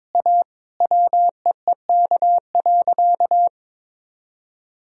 Morse code, heard as a keyed tone, about 21 wpm, 700 Hz. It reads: A WEEK.
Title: Sentences from Top 1000 Words - Review Part 2 - 22wpm